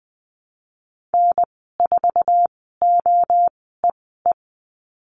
Morse code, heard as a keyed tone, about 20 wpm, spaced as for 14 wpm, 700 Hz. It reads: N4OEE
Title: 076 Lesson 13g - Letter L - Call Signs 20wpm